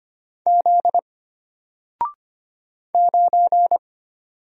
Morse code, heard as a keyed tone, about 25 wpm, 700 Hz. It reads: Z 9
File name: Single-Letter-Number-25wpm